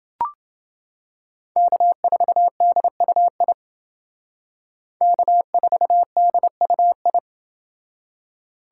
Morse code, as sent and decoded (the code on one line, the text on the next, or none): -.- ....- -.. ..- .. / -.- ....- -.. ..- ..
K4DUI K4DUI